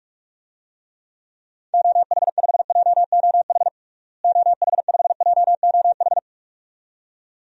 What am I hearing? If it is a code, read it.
Morse code, heard as a keyed tone, about 42 wpm, 700 Hz. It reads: OH5JOH OH5JOH
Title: DXCC-100 Call Signs - 6 Characters - Repeated 2x - 45wpm